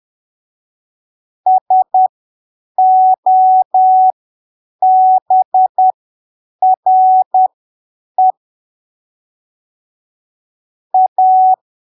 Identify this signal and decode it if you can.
Morse code, heard as a keyed tone, about 10 wpm, spaced as for 7 wpm, 750 Hz. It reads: SOBRE A
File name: arquivos / texto / texto 24 10ppm